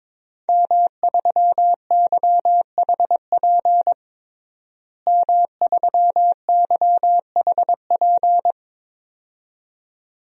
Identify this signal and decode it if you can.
Morse code, heard as a keyed tone, about 22 wpm, 700 Hz. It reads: M3YHP M3YHP
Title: DXCC-100 Call Signs - 5 Characters - Repeated 2x - 22wpm